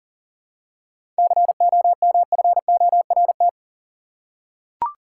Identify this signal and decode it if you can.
Morse code, heard as a keyed tone, about 40 wpm, 700 Hz. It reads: COMFORT
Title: Seven Letter Words - 40wpm